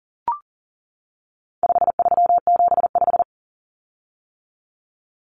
Morse code, heard as a keyed tone, about 40 wpm, 700 Hz. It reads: 5375
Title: Common Cut Numbers - 4-Digit 40wpm